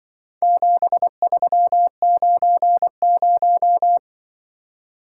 Morse code, heard as a keyed tone, about 24 wpm, 700 Hz. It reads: 7390
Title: Speed-Racing - Four Letter-Number Combinations - 20wpm